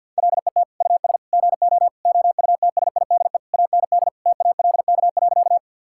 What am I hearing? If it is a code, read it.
Morse code, heard as a keyed tone, about 50 wpm, 700 Hz. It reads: LET US GO OUTSIDE AND TALK.
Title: Sentences from Top 1000 Words - Review Part 6 - 50wpm